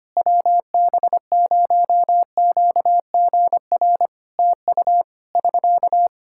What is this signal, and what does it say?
Morse code, heard as a keyed tone, about 25 wpm, 700 Hz. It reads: WB0QGR TU <SK>
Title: Practice QSO - Exchange 4 - Ending - 25wpm